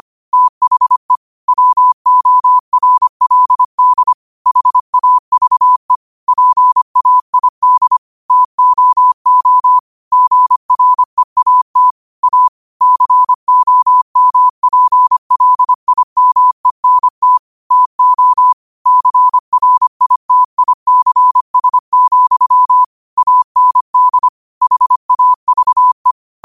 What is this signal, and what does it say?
Morse code, heard as a keyed tone, about 25 wpm, 1000 Hz. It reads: THE WORLD HAVE PAID TOO GREAT A COMPLIMENT TO CRITICS, AND HAVE
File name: qotdfiles / 25wpm